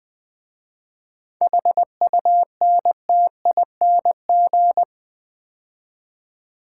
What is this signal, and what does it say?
Morse code, heard as a keyed tone, about 20 wpm, 700 Hz. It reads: HUNTING
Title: 7 Letter Words - First 3 Letters Given - 20wpm